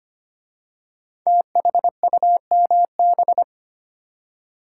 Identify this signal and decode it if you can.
Morse code, heard as a keyed tone, about 25 wpm, 700 Hz. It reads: THUMB
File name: 5 Letter Words - First 3 Letters Given - 25wpm